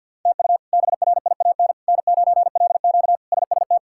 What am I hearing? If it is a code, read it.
Morse code, heard as a keyed tone, about 53 wpm, 700 Hz. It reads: TU BRIAN N9LQ SST
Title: K1USN SST QSO Practice - 50wpm